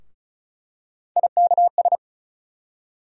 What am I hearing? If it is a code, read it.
Morse code, heard as a keyed tone, about 36 wpm, 700 Hz. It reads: IKS
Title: USA Call Sign Postfixes - 3 Letters 35wpm